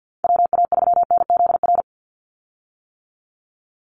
Morse code, heard as a keyed tone, about 50 wpm, 700 Hz. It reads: WA3NZR
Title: Call Signs - 50wpm